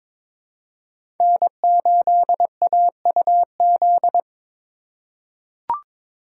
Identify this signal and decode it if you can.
Morse code, heard as a keyed tone, about 22 wpm, 700 Hz. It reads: N8AUZ
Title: USA Call Signs - 5 Characters - 2x Character Spacing - 22wpm